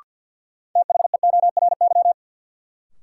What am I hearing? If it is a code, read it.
Morse code, heard as a keyed tone, about 52 wpm, 700 Hz. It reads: THEORY
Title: Sentences from Top 600 Words - New Word Drill - 50wpm